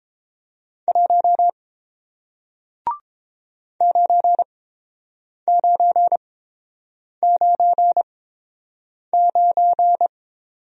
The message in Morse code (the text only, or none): 1 9999